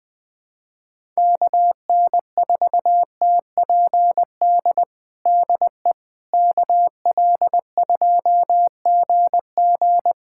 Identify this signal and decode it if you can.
Morse code, heard as a keyed tone, about 21 wpm, 700 Hz. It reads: KN4TPD DE KL2GG
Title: USA Callsign DE Callsign 20wpm